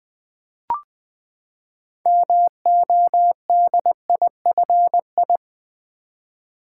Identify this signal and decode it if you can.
Morse code, heard as a keyed tone, about 20 wpm, 700 Hz. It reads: MODIFI
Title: Build A Word - 6 Letters At A Time - 12-Letter Words - 20wpm